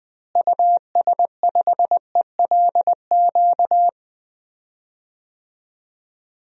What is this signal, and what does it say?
Morse code, heard as a keyed tone, about 20 wpm, 700 Hz. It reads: US5ELQ
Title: DXCC-100 Call Signs - 6 Characters - Repeated 2x - 20wpm